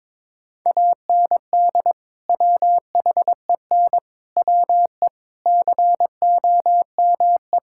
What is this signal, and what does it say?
Morse code, heard as a keyed tone, about 21 wpm, 700 Hz. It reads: AND WHEN WE COME
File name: practice / 22wpm / Build A Phrase - 5-Word Phrases - 22wpm